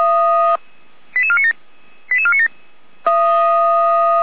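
Five-tone selective call (ZVEI), heard as quick runs of five tones, tones of about 70 ms. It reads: 88497 88497
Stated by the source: ZVEI five-tone call received by SDR.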